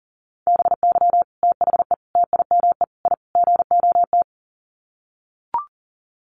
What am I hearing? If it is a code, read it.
Morse code, heard as a keyed tone, about 40 wpm, 700 Hz. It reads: BY THE TIME I GOT
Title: Sets of 5 Words Encore2 40wpm